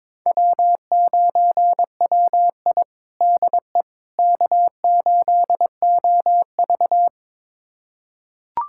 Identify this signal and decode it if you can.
Morse code, heard as a keyed tone, about 22 wpm, 700 Hz. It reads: W9WI DE K8OV
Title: USA Callsign DE Callsign - 4 Characters - 22wpm